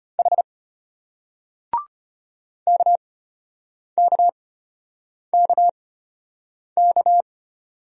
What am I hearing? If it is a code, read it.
Morse code, heard as a keyed tone, about 33 wpm, spaced as for 8 wpm, 700 Hz. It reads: H KKKK